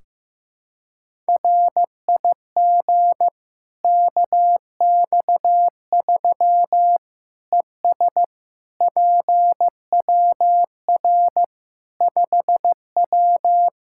Morse code, heard as a keyed tone, about 15 wpm, 700 Hz. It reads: RIG KX3 ES PWR 5W
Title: Practice QSO - Exchange 2 - Info - 15wpm